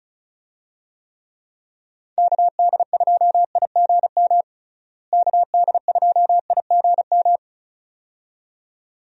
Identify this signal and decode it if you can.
Morse code, heard as a keyed tone, about 36 wpm, 700 Hz. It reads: KD2IGM KD2IGM
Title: USA Call Signs - 6 Characters - Repeated 2x - 35wpm